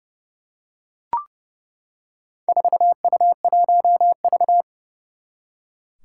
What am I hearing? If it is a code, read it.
Morse code, heard as a keyed tone, about 30 wpm, 690 Hz. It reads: 4U1V
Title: Four Challenging Letter-Number Combinations 30wpm